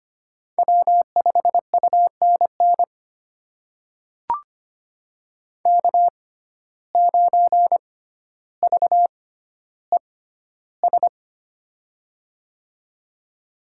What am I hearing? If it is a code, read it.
Morse code, heard as a keyed tone, about 25 wpm, spaced as for 13 wpm, 700 Hz. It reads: W5UNN K9VES
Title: USA Call Signs - 5 Characters - 6x Character Spacing - 25wpm